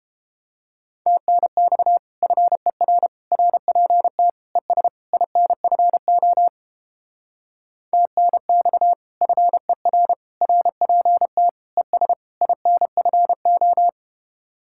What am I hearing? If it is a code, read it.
Morse code, heard as a keyed tone, about 29 wpm, 700 Hz. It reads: TNX FER RPT ES INFO TNX FER RPT ES INFO